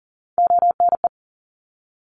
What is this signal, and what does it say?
Morse code, heard as a keyed tone, about 40 wpm, 700 Hz. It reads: ONE
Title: ARRL Sections 40wpm